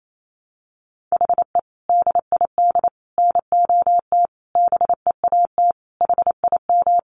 Morse code, heard as a keyed tone, about 28 wpm, 700 Hz. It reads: HE DID NOT BEAT HIM
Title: Sentences from Top 1300 Words 28wpm